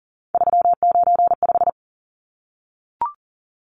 Morse code, heard as a keyed tone, about 40 wpm, 700 Hz. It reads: 395
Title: Numbers - 3-Digit 40wpm